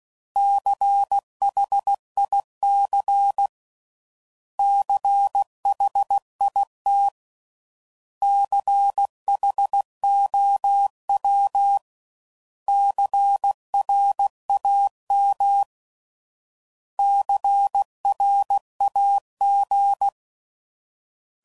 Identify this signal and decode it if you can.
Morse code, heard as a keyed tone, about 16 wpm, 800 Hz. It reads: CHIC CHIT CHOW CRAM CRAG
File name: Prefixes / By First Letter / Cs